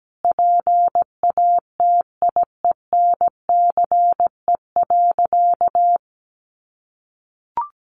Morse code, heard as a keyed tone, about 17 wpm, 700 Hz. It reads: PATIENCE.
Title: Sentences from Top 1300 Words 17wpm